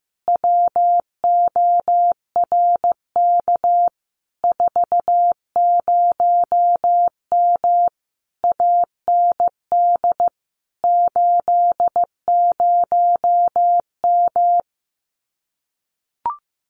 Morse code, heard as a keyed tone, about 15 wpm, 700 Hz. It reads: WORK 40M AND 80M
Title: Common QSO Phrases Encore 15wpm